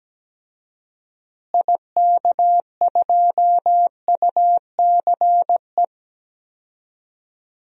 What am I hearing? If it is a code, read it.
Morse code, heard as a keyed tone, about 17 wpm, 700 Hz. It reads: IK2UCE
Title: DXCC-100 Call Signs - 6 Characters - Repeated 3x - 17wpm